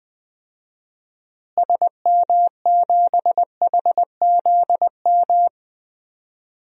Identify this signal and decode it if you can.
Morse code, heard as a keyed tone, about 21 wpm, 700 Hz. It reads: SM7HZM
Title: DXCC-100 Call Signs - 6 Characters - Repeated 3x - 20wpm